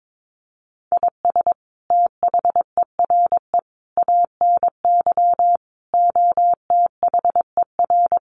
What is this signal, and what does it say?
Morse code, heard as a keyed tone, about 22 wpm, 700 Hz. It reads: IS THERE ANY OTHER
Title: Sentences from Top 1100 Words 22wpm